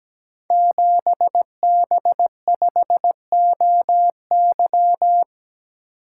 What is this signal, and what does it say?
Morse code, heard as a keyed tone, about 17 wpm, 700 Hz. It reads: 7B5OY
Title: Code Groups - 5 Letter:Numbers - 17wpm